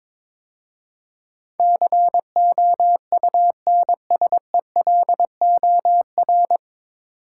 Morse code, heard as a keyed tone, about 22 wpm, 700 Hz. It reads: COUNSELOR
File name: 9 Letter Words - First 3 Letters Given - 22wpm